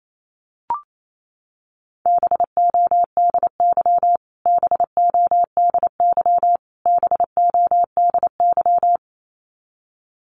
Morse code, heard as a keyed tone, about 28 wpm, 700 Hz. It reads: BODY BODY BODY
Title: Top 500 Words Repeated 28wpm